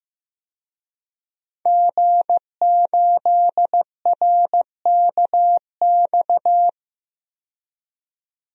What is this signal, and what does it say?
Morse code, heard as a keyed tone, about 15 wpm, 700 Hz. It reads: G8RKX
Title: DXCC-100 Call Signs - 5 Characters - 15wpm